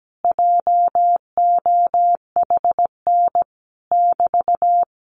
Morse code, heard as a keyed tone, about 17 wpm, 700 Hz. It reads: JOHN <BT>
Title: Common QSO Phrases Encore 17wpm